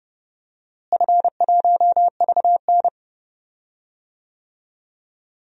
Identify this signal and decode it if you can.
Morse code, heard as a keyed tone, about 32 wpm, 700 Hz. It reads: F1VN
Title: DXCC-100 Call Signs - 4 Characters - Repeated 3x - 30wpm